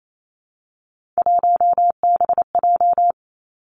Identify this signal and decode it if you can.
Morse code, heard as a keyed tone, about 28 wpm, 700 Hz. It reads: 1BJ
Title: Three Challenging Letter-Number Combinations 28wpm